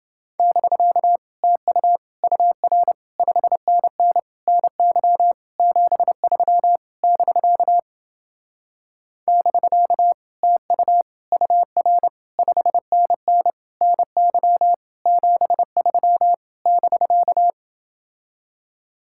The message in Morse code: <BK> TU UR 5NN NY 73 <BK> <BK> TU UR 5NN NY 73 <BK>

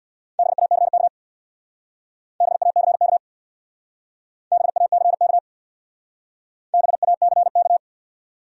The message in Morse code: BACK BACK BACK BACK

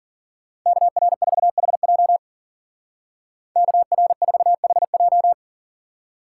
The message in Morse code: KR4HJ KR4HJ

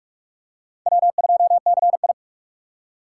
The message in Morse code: W2CI

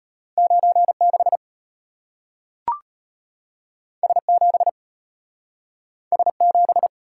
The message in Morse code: 96 S7 S7

